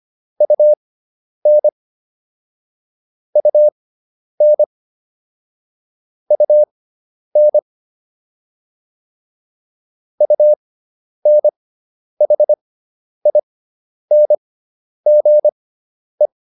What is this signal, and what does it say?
Morse code, heard as a keyed tone, about 25 wpm, spaced as for 10 wpm, 600 Hz. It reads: UN UN UN UNHINGE